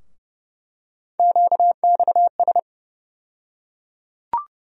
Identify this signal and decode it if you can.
Morse code, heard as a keyed tone, about 30 wpm, 700 Hz. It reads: QXS
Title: USA Call Sign Postfixes - 3 Letters 30wpm